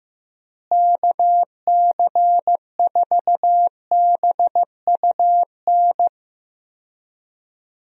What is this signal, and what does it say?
Morse code, heard as a keyed tone, about 15 wpm, 700 Hz. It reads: KC4BUN
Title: USA Call Signs - 6 Characters - Repeated 2x - 15wpm